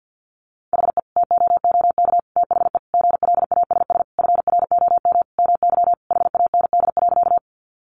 Morse code, heard as a keyed tone, about 50 wpm, 700 Hz. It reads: HE TOOK THE GLASS FROM MY HAND.
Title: Sentences from Top 1300 Words 50wpm